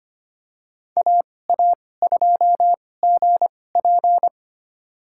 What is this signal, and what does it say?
Morse code, heard as a keyed tone, about 25 wpm, spaced as for 18 wpm, 700 Hz. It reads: AA2GP
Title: USA Call Signs - 5 Characters - 2x Character Spacing - 25wpm